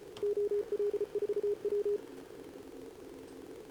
Morse code, heard as a keyed tone, about 30 wpm, 410 Hz. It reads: OL4W
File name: CQWWCW2025 / Sobota 15m ZD7VJ